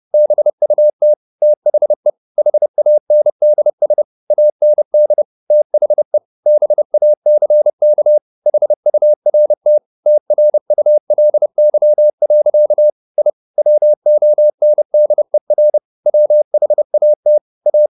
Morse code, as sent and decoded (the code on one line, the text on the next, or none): -... ..- - / - .... / .... .- -. -.. ... / .- -. -.. / - .... / -... .- -.-. -.- / .... ..- .-. - / - .-. ..- .-.. -.-- .-.-.- / .. / .-- --- -. -.. .-. / .-- .... .- - / .-
BUT THE HANDS AND THE BACK HURT TRULY. I WONDER WHAT A